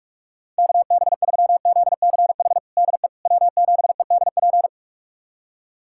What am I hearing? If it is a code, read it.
Morse code, heard as a keyed tone, about 42 wpm, 700 Hz. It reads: KB3ZCH DE W7EDP
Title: USA Callsign DE Callsign 45wpm